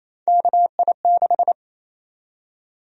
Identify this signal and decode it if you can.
Morse code, heard as a keyed tone, about 28 wpm, 700 Hz. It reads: KI6
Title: USA Call Sign Prefixes - 28wpm